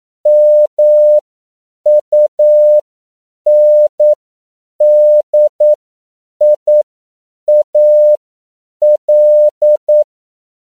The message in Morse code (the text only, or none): MUNDIAL